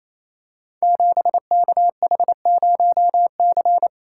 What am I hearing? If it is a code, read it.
Morse code, heard as a keyed tone, about 28 wpm, 700 Hz. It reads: 7KH0C
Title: Code Groups - 5 Letter:Numbers - 28wpm